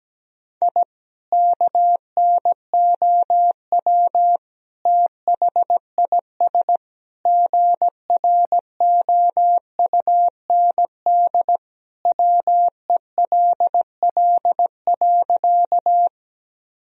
Morse code, heard as a keyed tone, about 17 wpm, 700 Hz. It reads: I KNOW THIS GROUND WELL.
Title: Sentences from Top 1000 Words - Review Part 2 - 17wpm